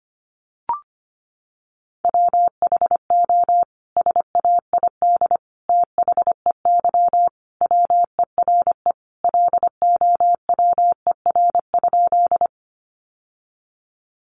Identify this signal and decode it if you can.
Morse code, heard as a keyed tone, about 25 wpm, 700 Hz. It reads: WHO SAID THEY WERE LOWER?
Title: Sentences from Top 700 Words 25wpm